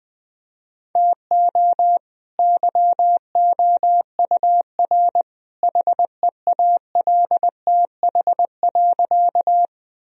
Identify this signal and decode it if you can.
Morse code, heard as a keyed tone, about 20 wpm, 700 Hz. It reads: TO YOUR HEALTH.
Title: Sentences from Top 1000 Words - Review Part 3 - 20wpm